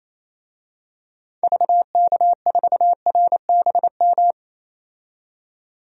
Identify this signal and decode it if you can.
Morse code, heard as a keyed tone, about 28 wpm, 700 Hz. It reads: VK4RBM
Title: DXCC-100 Call Signs - 6 Characters - 28wpm